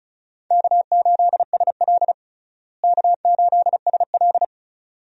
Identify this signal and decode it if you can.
Morse code, heard as a keyed tone, about 36 wpm, 700 Hz. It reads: K8SL K8SL
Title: USA Call Signs - 4 Characters - Repeated 2x - 35wpm